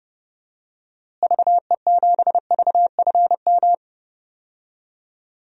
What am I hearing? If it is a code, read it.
Morse code, heard as a keyed tone, about 31 wpm, 700 Hz. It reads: VE7VFM
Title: DXCC-100 Call Signs - 6 Characters - Repeated 3x - 30wpm